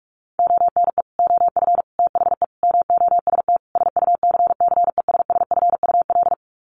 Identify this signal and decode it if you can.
Morse code, heard as a keyed tone, about 44 wpm, 700 Hz. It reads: ONE OF THE MOST SUCCESSFUL
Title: Sets of 5 Words Encore2 45wpm